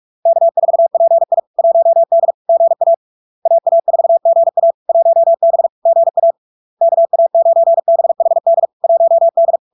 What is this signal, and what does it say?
Morse code, heard as a keyed tone, about 46 wpm, 670 Hz. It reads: K4PI 1D GA AA4GA 1B GA KA9BHD 1D
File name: LARK-FD-2022-Log-in-CW-45wpm-670-Hz